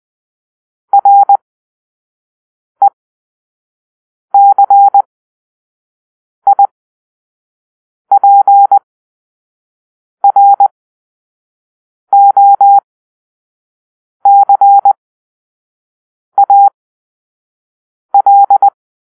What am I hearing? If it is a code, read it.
Morse code, heard as a keyed tone, about 20 wpm, spaced as for 5 wpm, 800 Hz. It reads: RECIPROCAL
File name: Grp-3 Drill-1